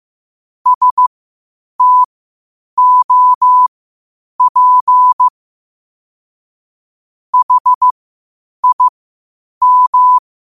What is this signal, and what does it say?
Morse code, heard as a keyed tone, about 15 wpm, spaced as for 8 wpm, 1000 Hz. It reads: STOP HIM
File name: qotdfiles / 8wpm